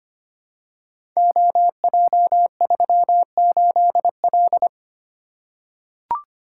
Morse code, heard as a keyed tone, about 26 wpm, 700 Hz. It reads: OJ38L
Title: Code Groups - 5 Letter:Numbers - 25wpm